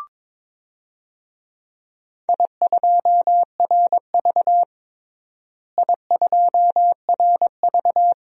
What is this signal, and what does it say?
Morse code, heard as a keyed tone, about 22 wpm, 700 Hz. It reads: I2RV I2RV
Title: DXCC-100 Call Signs - 4 Characters - Repeated 2x - 22wpm